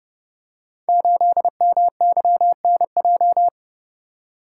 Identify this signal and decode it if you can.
Morse code, heard as a keyed tone, about 30 wpm, 700 Hz. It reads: 8MYNJ
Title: Code Groups - 5 Letter:Numbers - 30wpm